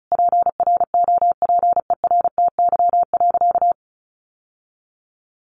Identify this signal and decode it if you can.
Morse code, heard as a keyed tone, about 34 wpm, 700 Hz. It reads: PROPERTY.
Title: Sentences from Top 1000 Words 35wpm